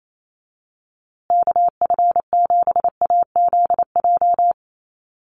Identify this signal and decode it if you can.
Morse code, heard as a keyed tone, about 28 wpm, 700 Hz. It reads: KF7AZJ
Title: USA Call Signs Repeated 3x 28wpm